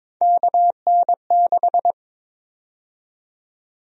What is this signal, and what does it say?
Morse code, heard as a keyed tone, about 22 wpm, 700 Hz. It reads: KN6
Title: USA Call Sign Prefixes - 22wpm